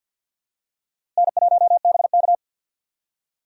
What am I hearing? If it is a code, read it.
Morse code, heard as a keyed tone, about 52 wpm, 700 Hz. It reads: N1BK
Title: USA Call Signs - 4 Characters - Repeated 2x - 50wpm